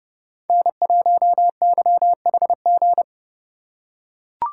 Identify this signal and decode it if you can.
Morse code, heard as a keyed tone, about 30 wpm, 700 Hz. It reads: N1YHG
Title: USA Call Signs - 5 Characters - 4x Character Spacing - 30wpm